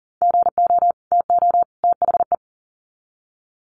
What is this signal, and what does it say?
Morse code, heard as a keyed tone, about 40 wpm, 700 Hz. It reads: GO TO THE
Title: Sets of 3 Words 40wpm